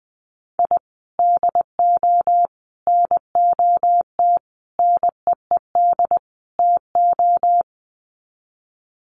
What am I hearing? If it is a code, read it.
Morse code, heard as a keyed tone, about 20 wpm, 700 Hz. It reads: I DO NOT NEED TO
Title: Sets of 5 Words 20wpm